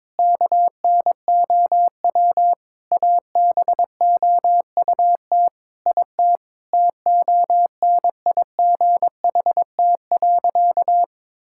Morse code, heard as a keyed tone, about 22 wpm, 700 Hz. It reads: KNOW ABOUT IT TONIGHT.
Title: Sentences from Top 1000 Words - Review Part 5 - 22wpm